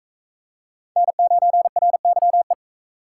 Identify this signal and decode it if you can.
Morse code, heard as a keyed tone, about 44 wpm, 700 Hz. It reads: N9RYE